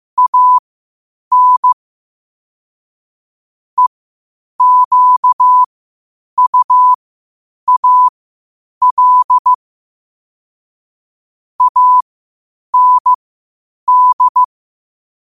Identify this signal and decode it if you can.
Morse code, heard as a keyed tone, about 15 wpm, spaced as for 8 wpm, 1000 Hz. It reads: AN EQUAL AND
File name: qotdfiles / 8wpm